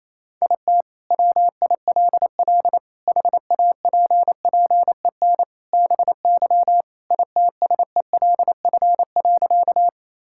Morse code, heard as a keyed tone, about 28 wpm, 700 Hz. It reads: IT WILL HAPPEN BY ITSELF.
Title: Sentences from Top 1000 Words - Review Part 5 - 28wpm